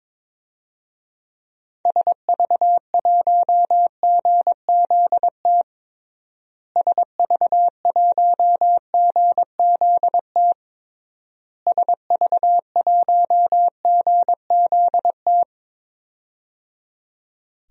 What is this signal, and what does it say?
Morse code, heard as a keyed tone, about 22 wpm, 700 Hz. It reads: SV1GZT SV1GZT SV1GZT
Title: DXCC-100 Call Signs - 6 Characters - Repeated 3x - 22wpm